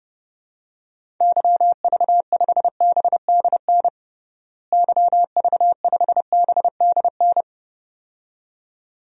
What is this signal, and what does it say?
Morse code, heard as a keyed tone, about 30 wpm, 700 Hz. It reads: YV5BDN YV5BDN
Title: DXCC-100 Call Signs - 6 Characters - Repeated 2x - 30wpm